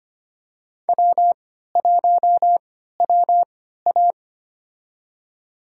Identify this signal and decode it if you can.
Morse code, heard as a keyed tone, about 25 wpm, spaced as for 14 wpm, 700 Hz. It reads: W1WA
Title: USA Call Signs - 4 Characters - 3x Character Spacing - 25wpm